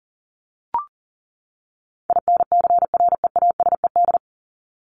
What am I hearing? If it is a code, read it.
Morse code, heard as a keyed tone, about 39 wpm, 700 Hz. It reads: INCREASED
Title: ED Suffix - Top 100 Words with Base Words 40wpm